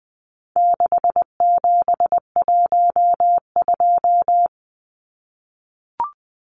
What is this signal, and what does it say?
Morse code, heard as a keyed tone, about 20 wpm, 700 Hz. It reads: 6712
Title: Numbers - 4-Digit 20wpm